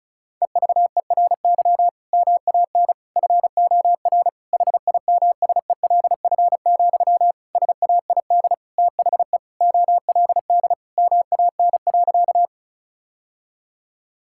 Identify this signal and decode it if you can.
Morse code, heard as a keyed tone, about 36 wpm, 700 Hz. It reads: EVERY MAN FOR HIMSELF, SAID THE OLD MAN.
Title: Sentences from Top 1600 Words 35wpm